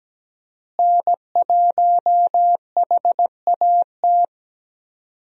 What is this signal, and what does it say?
Morse code, heard as a keyed tone, about 17 wpm, 700 Hz. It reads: N1HAT
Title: USA Call Signs - 5 Characters - Repeated 2x - 17wpm